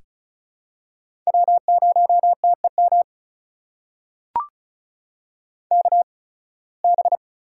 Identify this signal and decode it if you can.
Morse code, heard as a keyed tone, about 36 wpm, spaced as for 23 wpm, 700 Hz. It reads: W0TEM KB
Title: USA Call Signs - 5 Characters - 8x Character Spacing - 35wpm